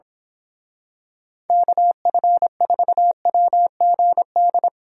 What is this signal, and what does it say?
Morse code, heard as a keyed tone, about 25 wpm, 700 Hz. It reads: KF4WGD
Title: Speed Racing - USA 2x3 Call Signs 22wpm